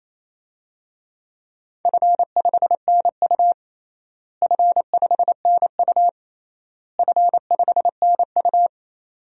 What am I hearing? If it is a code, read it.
Morse code, heard as a keyed tone, about 28 wpm, 700 Hz. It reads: F5NU F5NU F5NU
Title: DXCC-100 Call Signs - 4 Characters - Repeated 3x - 28wpm